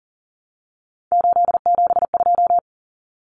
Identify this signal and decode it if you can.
Morse code, heard as a keyed tone, about 39 wpm, 700 Hz. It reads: 872
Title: Common Cut Numbers - 3-Digit 40wpm